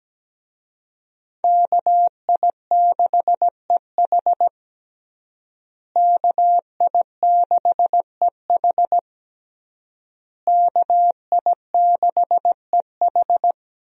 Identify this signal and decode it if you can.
Morse code, heard as a keyed tone, about 17 wpm, 700 Hz. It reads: KI6EH KI6EH KI6EH
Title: USA Call Signs - 5 Characters - Repeated 3x - 17wpm